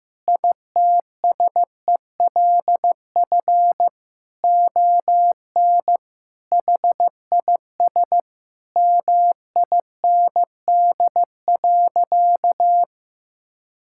Morse code, heard as a keyed tone, about 15 wpm, 700 Hz. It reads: ITSELF ON HIS MIND.
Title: Sentences from Top 1600 Words 15wpm